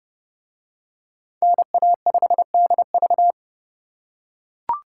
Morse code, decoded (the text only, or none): NA5DV